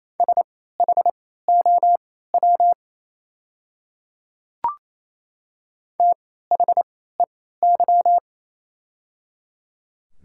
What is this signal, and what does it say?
Morse code, heard as a keyed tone, about 28 wpm, spaced as for 16 wpm, 700 Hz. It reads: SHOW THEY
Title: 3x Character Spacing - Top 200 Words - 28wpm